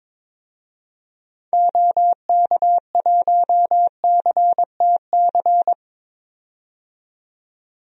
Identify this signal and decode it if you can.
Morse code, heard as a keyed tone, about 22 wpm, 700 Hz. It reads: OK1CTC
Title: DXCC-100 Call Signs - 6 Characters - 22wpm